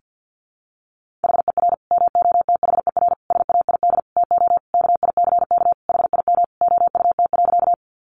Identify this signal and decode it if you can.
Morse code, heard as a keyed tone, about 50 wpm, 700 Hz. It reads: HER MOTHER SAID TO KICK HIM OUT.
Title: Sentences from Top 1300 Words 50wpm